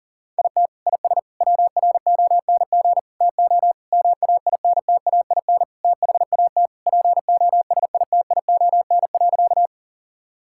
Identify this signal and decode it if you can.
Morse code, heard as a keyed tone, about 40 wpm, 700 Hz. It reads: IT IS WRONG TO MAINTAIN THAT POSITION.
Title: Sentences from Top 1000 Words - Review Part 5 - 40wpm